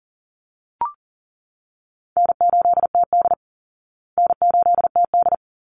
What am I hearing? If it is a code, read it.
Morse code, heard as a keyed tone, about 40 wpm, 700 Hz. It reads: N8TD N8TD
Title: Call Signs Repeated 3x - 40wpm